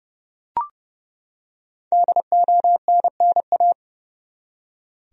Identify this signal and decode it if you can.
Morse code, heard as a keyed tone, about 30 wpm, 700 Hz. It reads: DONNA
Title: USA Common Names - Max 5 Letters - 30wpm